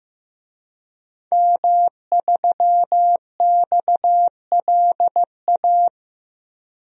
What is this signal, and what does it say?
Morse code, heard as a keyed tone, about 15 wpm, 700 Hz. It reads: M3XLA
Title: DXCC-100 Call Signs - 5 Characters - 15wpm